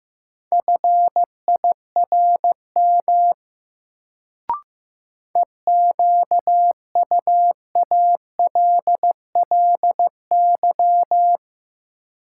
Morse code, heard as a keyed tone, about 15 wpm, 700 Hz. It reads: FIRM EQUALLY
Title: Sentences from Top 1400 Words - New Word Drill Alt - 15wpm